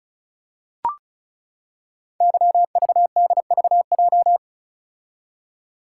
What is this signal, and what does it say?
Morse code, heard as a keyed tone, about 36 wpm, 690 Hz. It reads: YVDVJ
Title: Code Groups - 5 Letters - 35wpm